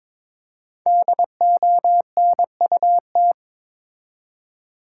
Morse code, heard as a keyed tone, about 22 wpm, 700 Hz. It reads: DONUT
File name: Difficult Words with Hints - 22wpm